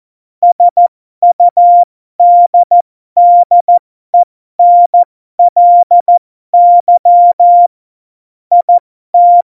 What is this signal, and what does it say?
Morse code, heard as a keyed tone, about 14 wpm, spaced as for 12 wpm, 700 Hz. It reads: SUDDENLY IT
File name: Morsekurs / Witze